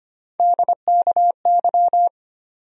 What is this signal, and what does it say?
Morse code, heard as a keyed tone, about 25 wpm, 700 Hz. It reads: DKY
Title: USA Call Sign Postfixes - 3 Letters 25wpm